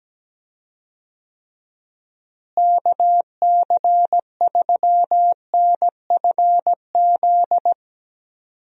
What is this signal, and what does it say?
Morse code, heard as a keyed tone, about 17 wpm, 700 Hz. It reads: KC3NFZ
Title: USA Call Signs - 6 Characters - Repeated 3x - 17wpm